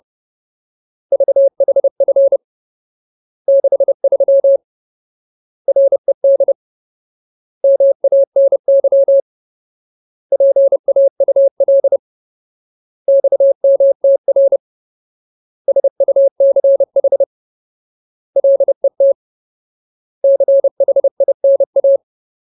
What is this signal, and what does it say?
Morse code, heard as a keyed tone, about 30 wpm, 550 Hz. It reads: VHF 63 RED MANY PAUL XMTR SUCH LET CHINA
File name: Ultimate Word List / 30WPM Ultimate